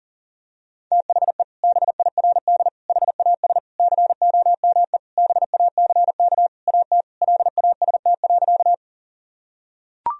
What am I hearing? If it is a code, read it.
Morse code, heard as a keyed tone, about 37 wpm, 700 Hz. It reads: THE BIRD HAS COME BACK AT LAST.
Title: Sentences from Top 1000 Words - Review Part 2 - 40wpm